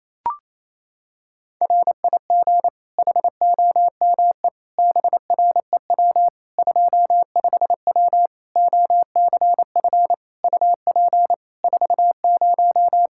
Morse code, heard as a keyed tone, about 28 wpm, 700 Hz. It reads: RIG HOME BREW 25W OCF UP 40
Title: Practice QSO - Exchange 2 - Info - 28wpm